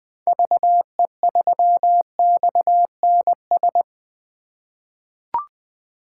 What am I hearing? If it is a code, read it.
Morse code, heard as a keyed tone, about 20 wpm, 700 Hz. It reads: VE3XNS
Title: 2023 POTA Activator Call Signs - 20wpm